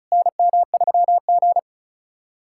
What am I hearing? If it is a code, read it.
Morse code, heard as a keyed tone, about 36 wpm, 700 Hz. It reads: NM3G
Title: SOTA QSO Practice 35wpm